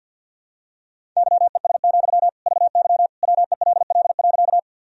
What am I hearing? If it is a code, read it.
Morse code, heard as a keyed tone, about 52 wpm, 700 Hz. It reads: YES, VY WELL.
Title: Sentences from Top 500 Words and Abbreviations 50wpm